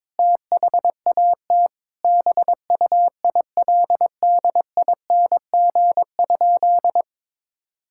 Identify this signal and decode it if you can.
Morse code, heard as a keyed tone, about 22 wpm, 700 Hz. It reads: THAT BUILDING?
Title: Sentences from Top 1000 Words - Review Part 5 - 22wpm